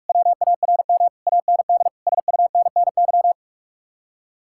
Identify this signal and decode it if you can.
Morse code, heard as a keyed tone, about 49 wpm, 700 Hz. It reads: WARM AND SUNNY